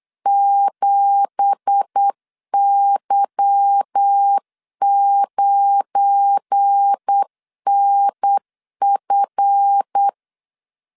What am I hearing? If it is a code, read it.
Morse code, heard as a keyed tone, about 8 wpm, 800 Hz. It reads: 7Y9NF
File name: 8wpm